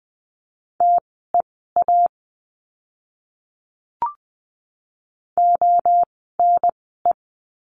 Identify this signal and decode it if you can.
Morse code, heard as a keyed tone, about 20 wpm, spaced as for 15 wpm, 700 Hz. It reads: TEA ONE
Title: 024 Lesson 5f - Letter o - Words 20wpm